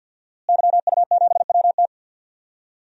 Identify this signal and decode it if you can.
Morse code, heard as a keyed tone, about 51 wpm, 700 Hz. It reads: YU7WT
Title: DXCC-100 Call Signs - 5 Characters - Repeated 2x - 50wpm